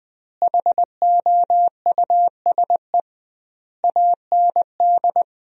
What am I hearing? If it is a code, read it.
Morse code, heard as a keyed tone, about 20 wpm, 700 Hz. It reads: HOUSE AND
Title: 2x Word-Spacing - Sentences from Top 400 Words - 20wpm